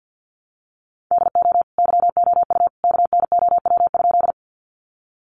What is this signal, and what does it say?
Morse code, heard as a keyed tone, about 50 wpm, 700 Hz. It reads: DO YOU KNOW?
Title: Sentences from Top 100 Words 50wpm